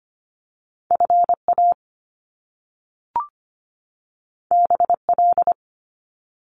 Morse code, heard as a keyed tone, about 25 wpm, 700 Hz. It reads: FA BL
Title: Top Two Letter Combinations in Words 25wpm